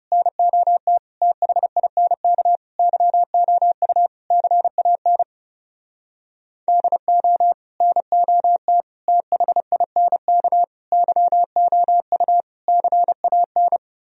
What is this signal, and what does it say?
Morse code, heard as a keyed tone, about 31 wpm, 700 Hz. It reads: NOT THINK YOU CAN DO NOT THINK YOU CAN
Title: Speed-Racing - Sets of 5 Words 30wpm